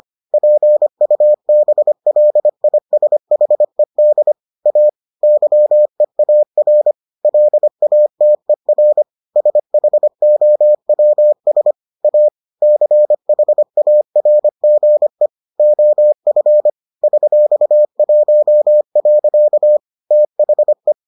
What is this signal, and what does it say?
Morse code, heard as a keyed tone, about 25 wpm, 600 Hz. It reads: PUBLISHED A YEAR LATER SHOWS A CHARGE OF $1. THE